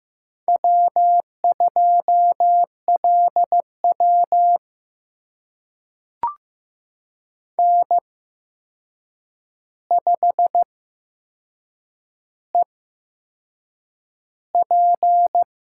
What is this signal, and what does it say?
Morse code, heard as a keyed tone, about 15 wpm, spaced as for 6 wpm, 700 Hz. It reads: W2LW N5EP
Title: USA Call Signs - 4 Characters - 8x Character Spacing - 15wpm